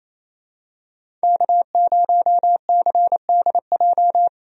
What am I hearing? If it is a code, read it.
Morse code, heard as a keyed tone, about 28 wpm, 700 Hz. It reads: K0CDJ
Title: USA Call Signs - 5 Characters - Repeated 2x - 28wpm